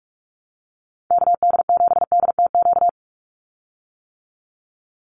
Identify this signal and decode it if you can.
Morse code, heard as a keyed tone, about 44 wpm, 700 Hz. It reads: KD7DTQ
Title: USA Call Signs 45wpm